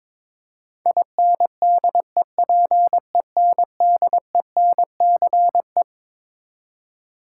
Morse code, heard as a keyed tone, about 21 wpm, 700 Hz. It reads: INDEPENDENCE
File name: Twelve Letter Words - 22wpm